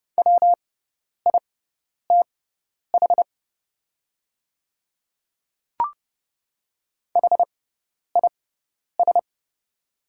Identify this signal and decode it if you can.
Morse code, heard as a keyed tone, about 30 wpm, spaced as for 10 wpm, 700 Hz. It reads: WITH HIS
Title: 6x Character Spacing - Top 100 Words - 30wpm